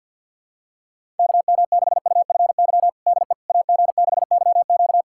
Morse code, heard as a keyed tone, about 50 wpm, 700 Hz. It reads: KM6UFY DE AG6YQ
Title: USA Callsign DE Callsign 50wpm